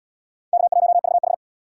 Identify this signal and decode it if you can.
Morse code, heard as a keyed tone, about 77 wpm, 700 Hz. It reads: K1VH